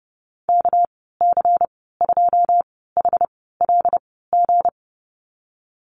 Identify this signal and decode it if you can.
Morse code, heard as a keyed tone, about 30 wpm, spaced as for 17 wpm, 700 Hz. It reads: KC2HLG
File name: Lesson 37g - Letter x - Call Signs 30wpm